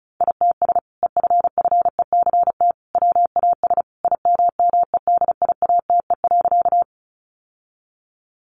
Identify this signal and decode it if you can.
Morse code, heard as a keyed tone, about 34 wpm, 700 Hz. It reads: ITS EFFECT WAS IMMEDIATE.
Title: Sentences from Top 1200 Words 35wpm